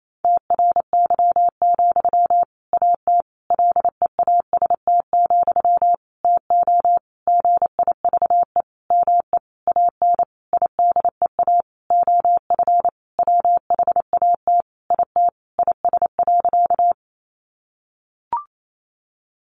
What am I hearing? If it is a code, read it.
Morse code, heard as a keyed tone, about 28 wpm, 700 Hz. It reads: TRY, AT LEAST, TO GIVE ME AN IDEA OF WHAT IT IS.
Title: Sentences from Top 700 Words 28wpm